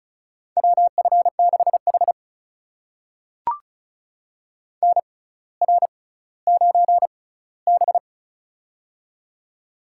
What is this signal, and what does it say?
Morse code, heard as a keyed tone, about 36 wpm, spaced as for 18 wpm, 700 Hz. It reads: WF6H NR9B
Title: USA Call Signs - 4 Characters - 6x Character Spacing - 35wpm